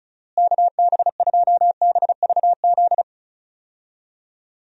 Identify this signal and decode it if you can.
Morse code, heard as a keyed tone, about 34 wpm, 700 Hz. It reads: KB2BVZ
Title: USA Call Signs - 6 Characters - Repeated 2x - 35wpm